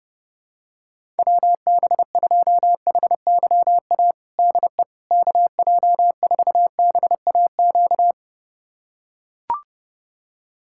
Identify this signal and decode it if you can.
Morse code, heard as a keyed tone, about 30 wpm, 700 Hz. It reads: WB2HYA DE KJ4BAQ
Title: USA Callsign DE Callsign - 6 Characters - 30wpm